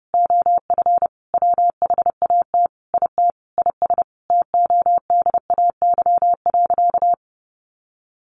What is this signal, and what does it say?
Morse code, heard as a keyed tone, about 30 wpm, 700 Hz. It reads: OF WHAT IT IS TODAY.
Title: Sentences from Top 1500 Words 30wpm